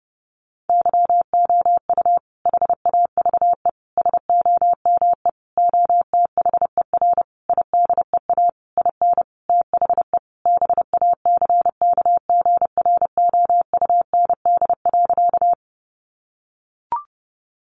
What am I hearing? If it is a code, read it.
Morse code, heard as a keyed tone, about 30 wpm, 700 Hz. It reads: YOU HAVE SOME OTHER IDEA IN THE BACKGROUND.
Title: Sentences from Top 1300 Words 30wpm